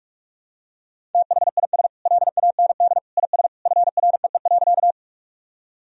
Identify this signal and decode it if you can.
Morse code, heard as a keyed tone, about 41 wpm, 700 Hz. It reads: THIS LAND IS FREE.
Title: Sentences from Top 1000 Words - Review Part 1 - 45wpm